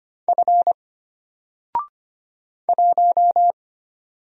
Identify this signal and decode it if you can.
Morse code, heard as a keyed tone, about 25 wpm, 700 Hz. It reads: F 1
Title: Single Letter-Number Alt 25wpm